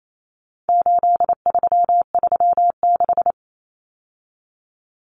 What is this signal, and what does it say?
Morse code, heard as a keyed tone, about 28 wpm, 700 Hz. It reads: 8336
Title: Numbers - 4-Digit 28wpm